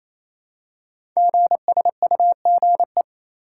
Morse code, heard as a keyed tone, about 27 wpm, 700 Hz. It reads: GSUGE
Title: Code Groups - 5 Letters - 28wpm